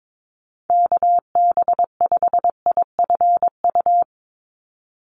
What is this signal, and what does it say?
Morse code, heard as a keyed tone, about 22 wpm, 700 Hz. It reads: KB5IFU
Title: USA Call Signs 22wpm